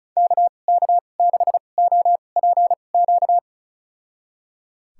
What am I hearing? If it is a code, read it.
Morse code, heard as a keyed tone, about 34 wpm, spaced as for 25 wpm, 700 Hz. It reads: KK6OPQ
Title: USA Call Signs - 2x Character Spacing - 35wpm